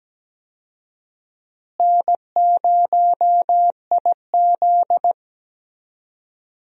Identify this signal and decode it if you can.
Morse code, heard as a keyed tone, about 17 wpm, 700 Hz. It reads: N0IZ
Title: USA Call Signs - 4 Characters - Repeated 3x - 17wpm